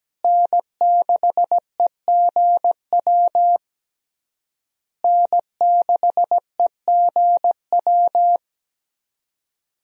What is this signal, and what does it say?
Morse code, heard as a keyed tone, about 17 wpm, 700 Hz. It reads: N6EGW N6EGW
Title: USA Call Signs - 5 Characters - Repeated 3x - 17wpm